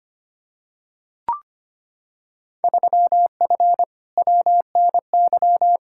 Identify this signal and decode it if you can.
Morse code, heard as a keyed tone, about 25 wpm, 700 Hz. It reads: 3F WNY
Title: Field Day Exchanges - 25wpm